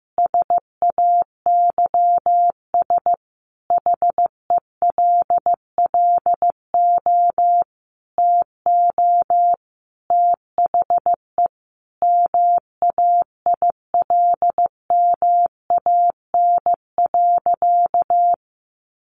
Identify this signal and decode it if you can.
Morse code, heard as a keyed tone, about 15 wpm, 700 Hz. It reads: SAYS HELLO TO THE MAILMAN.
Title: Easy Phrases 15wpm